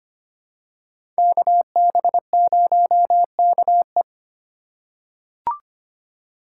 Morse code, heard as a keyed tone, about 25 wpm, 700 Hz. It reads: KB0KE
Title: USA Call Signs - 3x Character Spacing - 25wpm